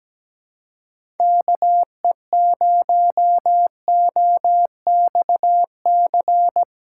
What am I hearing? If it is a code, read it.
Morse code, heard as a keyed tone, about 17 wpm, 700 Hz. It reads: KE0OXC
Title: USA Call Signs - 6x Character Spacing - 17wpm